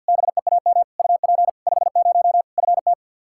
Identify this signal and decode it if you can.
Morse code, heard as a keyed tone, about 53 wpm, 700 Hz. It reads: BEAM UP 50 FT
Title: K1USN SST QSO Practice 50wpm